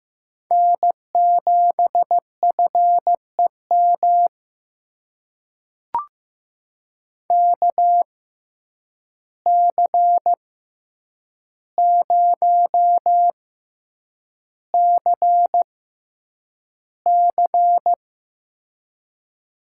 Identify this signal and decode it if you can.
Morse code, heard as a keyed tone, about 15 wpm, spaced as for 8 wpm, 700 Hz. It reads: N7FEM KC0CC
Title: USA Call Signs - 5 Characters - 6x Character Spacing - 15wpm